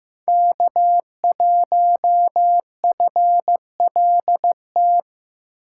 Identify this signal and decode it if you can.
Morse code, heard as a keyed tone, about 15 wpm, 700 Hz. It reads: K1FLT
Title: USA Call Signs - 5 Characters - 4x Character Spacing - 15wpm